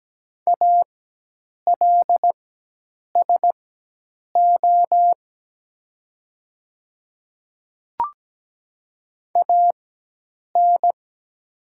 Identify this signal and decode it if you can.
Morse code, heard as a keyed tone, about 17 wpm, spaced as for 8 wpm, 690 Hz. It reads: ALSO AN
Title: 4x Character Spacing - Top 200 Words - 17wpm